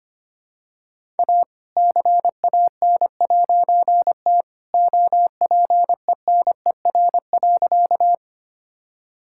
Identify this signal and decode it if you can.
Morse code, heard as a keyed tone, about 25 wpm, 700 Hz. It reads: A CAN'T OPENER.
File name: practice / 25wpm / All-Age Appropriate Jokes - 25wpm